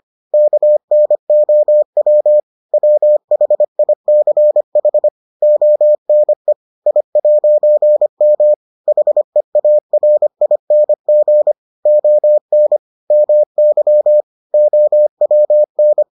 Morse code, heard as a keyed tone, about 25 wpm, 600 Hz. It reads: KNOW WHICH ONE I'M HEARING ON MY OWN